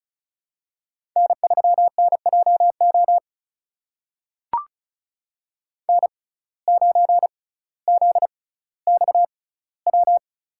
Morse code, heard as a keyed tone, about 36 wpm, spaced as for 18 wpm, 700 Hz. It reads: N3NJO N9ZXW
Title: USA Call Signs - 5 Characters - 6x Character Spacing - 35wpm